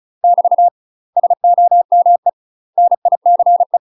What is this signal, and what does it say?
Morse code, heard as a keyed tone, about 34 wpm, 700 Hz. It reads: <BT> SOME NICE
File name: UK MASTER QSOs TRAINING/35WPM